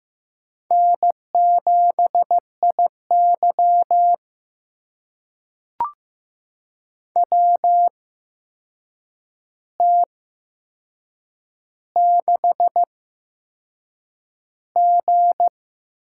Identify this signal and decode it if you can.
Morse code, heard as a keyed tone, about 15 wpm, spaced as for 6 wpm, 700 Hz. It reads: N7IY WT6G
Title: USA Call Signs - 8x Character Spacing - 15wpm